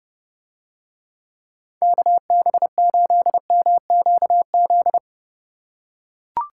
Morse code, heard as a keyed tone, about 30 wpm, 700 Hz. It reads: KB8MQZ